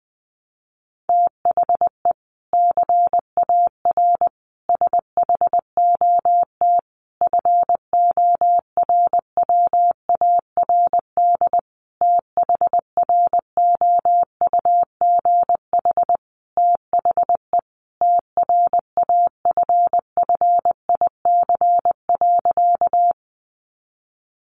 Morse code, heard as a keyed tone, about 20 wpm, 700 Hz. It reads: THE CAR SHOT FORWARD THROUGH THE TRAFFIC.
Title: Sentences from Top 1300 Words 20wpm